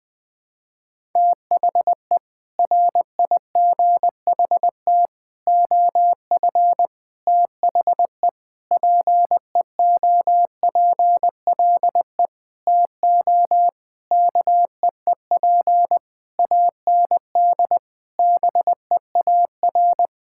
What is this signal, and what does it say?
Morse code, heard as a keyed tone, about 19 wpm, 700 Hz. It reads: THE RIGHT OF THE PEOPLE TO KEEP AND BEAR
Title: Sentences with 8-Word Phrases - 20wpm